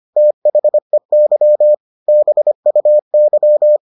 Morse code, heard as a keyed tone, about 25 wpm, 600 Hz. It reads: THEY BUY